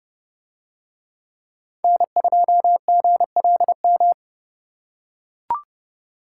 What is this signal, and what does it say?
Morse code, heard as a keyed tone, about 28 wpm, 700 Hz. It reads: N2GLM